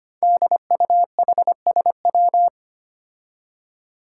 Morse code, heard as a keyed tone, about 25 wpm, 700 Hz. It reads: DUHSW
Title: Five Challenging Letter-Number Combinations 25wpm